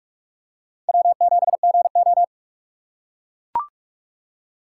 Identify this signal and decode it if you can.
Morse code, heard as a keyed tone, about 43 wpm, 700 Hz. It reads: W7GO
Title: USA Call Signs - 3x Character Spacing - 45wpm